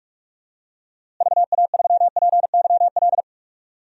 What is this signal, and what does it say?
Morse code, heard as a keyed tone, about 42 wpm, 700 Hz. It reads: VA3PYL
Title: DXCC-100 Call Signs - 6 Characters - Repeated 3x - 45wpm